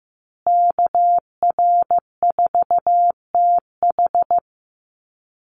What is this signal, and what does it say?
Morse code, heard as a keyed tone, about 15 wpm, 700 Hz. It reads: KR4TH
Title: Call Signs Repeated 3x - 15wpm